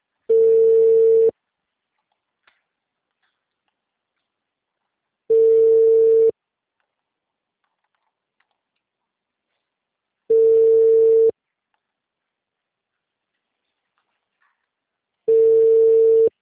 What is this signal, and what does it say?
Ringback tone (China, 450 Hz).
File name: out- -2030- - - .4436